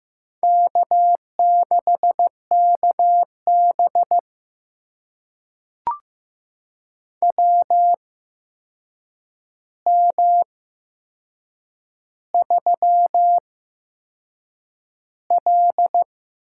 Morse code, heard as a keyed tone, about 15 wpm, spaced as for 6 wpm, 700 Hz. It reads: K6KB WM3L
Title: USA Call Signs - 4 Characters - 8x Character Spacing - 15wpm